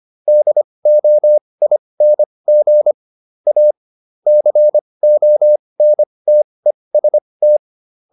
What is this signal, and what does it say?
Morse code, heard as a keyed tone, about 25 wpm, spaced as for 20 wpm, 600 Hz. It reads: DOING A CONTEST